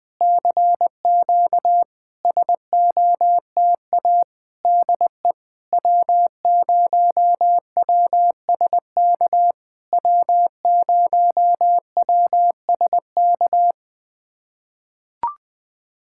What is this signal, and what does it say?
Morse code, heard as a keyed tone, about 20 wpm, 700 Hz. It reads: CQ SOTA DE W0WSK W0WSK
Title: SOTA QSO Practice 20wpm